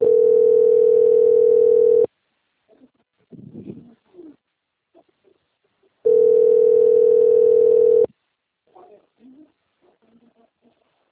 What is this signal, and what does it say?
Ringback tone (North America), 440 + 480 Hz.
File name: out- - - - - .29756